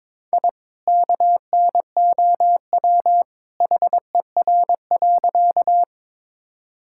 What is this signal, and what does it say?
Morse code, heard as a keyed tone, about 21 wpm, 700 Hz. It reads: I KNOW HER.
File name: practice / 22wpm / Sentences with 2-Word Phrases - 22wpm